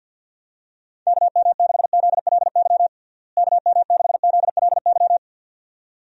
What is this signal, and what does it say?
Morse code, heard as a keyed tone, about 47 wpm, 700 Hz. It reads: KM6ZLY KM6ZLY
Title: USA Call Signs - 6 Characters - Repeated 2x - 50wpm